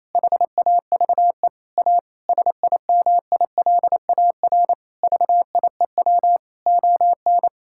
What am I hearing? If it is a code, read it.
Morse code, heard as a keyed tone, about 28 wpm, 700 Hz. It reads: HAVE A SIMILAR VIEW ON
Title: Sentences from Top 1000 Words - Review Part 6 - 28wpm